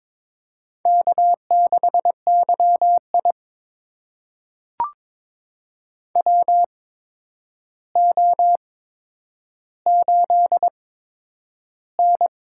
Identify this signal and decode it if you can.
Morse code, heard as a keyed tone, about 22 wpm, spaced as for 9 wpm, 700 Hz. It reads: K6YI WO8N
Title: USA Call Signs - 4 Characters - 8x Character Spacing - 22wpm